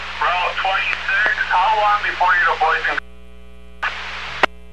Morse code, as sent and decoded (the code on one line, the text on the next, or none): none